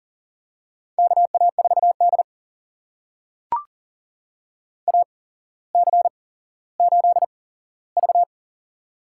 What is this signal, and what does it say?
Morse code, heard as a keyed tone, about 42 wpm, spaced as for 17 wpm, 700 Hz. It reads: KA4D AC8V
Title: USA Call Signs - 4 Characters - 8x Character Spacing - 40wpm